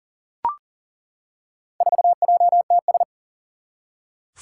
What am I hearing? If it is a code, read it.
Morse code, heard as a keyed tone, about 40 wpm, 700 Hz. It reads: 4JTS
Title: Four Challenging Letter-Number Combinations 40wpm